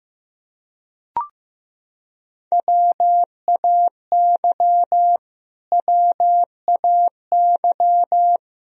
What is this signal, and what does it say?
Morse code, heard as a keyed tone, about 15 wpm, 700 Hz. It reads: WAY WAY
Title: Top 100 Words Repeated 15wpm